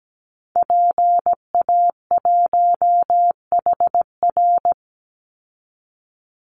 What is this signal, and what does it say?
Morse code, heard as a keyed tone, about 17 wpm, 700 Hz. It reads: PA1HR
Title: Call Signs Repeated 2x - 17wpm